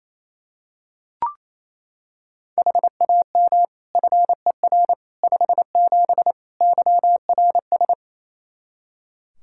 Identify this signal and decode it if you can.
Morse code, heard as a keyed tone, about 28 wpm, 700 Hz. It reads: HAM FER 57 YRS
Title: K1USN SST QSO Practice 28wpm